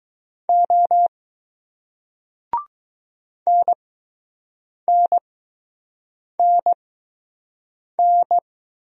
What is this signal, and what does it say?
Morse code, heard as a keyed tone, about 20 wpm, spaced as for 6 wpm, 700 Hz. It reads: O NNNN